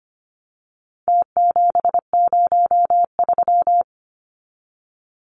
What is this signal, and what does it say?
Morse code, heard as a keyed tone, about 25 wpm, 700 Hz. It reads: T703
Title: Numbers - Contest Style - 4-Digit 25wpm